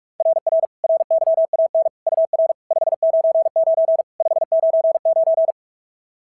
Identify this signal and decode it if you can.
Morse code, heard as a keyed tone, about 42 wpm, 660 Hz. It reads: RR RYAN UR 599 599
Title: SOTA QSO Practice 45wpm